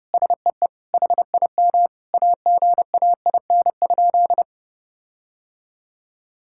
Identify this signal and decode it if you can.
Morse code, heard as a keyed tone, about 30 wpm, 700 Hz. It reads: SEE HIM AGAIN?
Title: Sentences from Top 1600 Words 30wpm